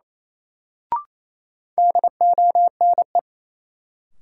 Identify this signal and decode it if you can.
Morse code, heard as a keyed tone, about 28 wpm, 700 Hz. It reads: DONE
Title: Sets of 5 Words Encore - New Word Drill - 28wpm